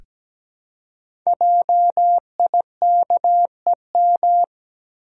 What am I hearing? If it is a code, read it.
Morse code, heard as a keyed tone, about 17 wpm, 700 Hz. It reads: JIKEM
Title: Code Groups - 5 Letters - 17wpm